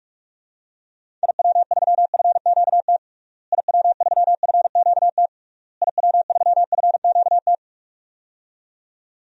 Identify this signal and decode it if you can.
Morse code, heard as a keyed tone, about 46 wpm, 700 Hz. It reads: IW3FQT IW3FQT IW3FQT
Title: DXCC-100 Call Signs - 6 Characters - Repeated 3x - 45wpm